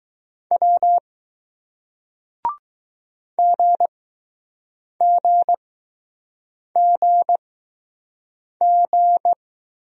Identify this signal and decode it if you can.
Morse code, heard as a keyed tone, about 21 wpm, spaced as for 6 wpm, 700 Hz. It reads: W GGGG